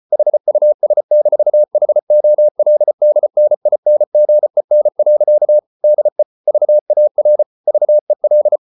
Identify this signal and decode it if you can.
Morse code, heard as a keyed tone, about 34 wpm, 600 Hz. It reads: HUS-HOLDNINGEN. DE VAR VEL
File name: Ferie QRQ 34wpm